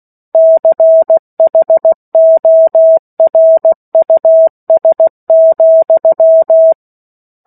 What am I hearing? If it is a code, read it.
Morse code, heard as a keyed tone, about 16 wpm, 650 Hz. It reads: CHORUS,
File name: 03-Chapter-03 16 WPM